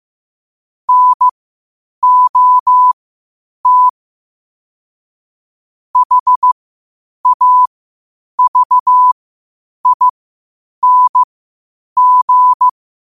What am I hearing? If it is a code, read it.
Morse code, heard as a keyed tone, about 15 wpm, spaced as for 8 wpm, 1000 Hz. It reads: NOT HAVING
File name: qotdfiles / 8wpm